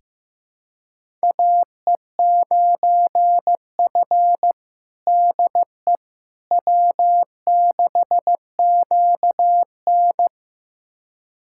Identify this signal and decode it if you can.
Morse code, heard as a keyed tone, about 15 wpm, 700 Hz. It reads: AE9F DE W6QN
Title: USA Callsign DE Callsign - 4 Characters - 15wpm